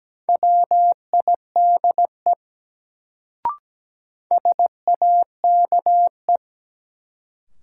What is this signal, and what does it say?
Morse code, heard as a keyed tone, about 17 wpm, 700 Hz. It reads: WIDE SAKE
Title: Sets of 4 Words - New Word Drill - 17wpm